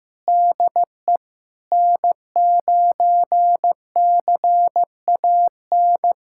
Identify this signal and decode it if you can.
Morse code, heard as a keyed tone, about 16 wpm, 700 Hz. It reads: DE N9CAN
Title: USA Callsign DE Callsign - 5 Characters - 15wpm